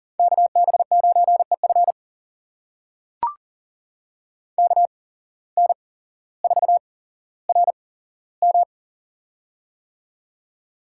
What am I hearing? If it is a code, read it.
Morse code, heard as a keyed tone, about 39 wpm, spaced as for 17 wpm, 700 Hz. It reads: KB9EF KN4RM
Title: USA Call Signs - 5 Characters - 8x Character Spacing - 40wpm